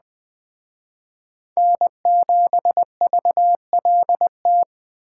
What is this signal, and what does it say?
Morse code, heard as a keyed tone, about 19 wpm, 700 Hz. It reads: N7VLT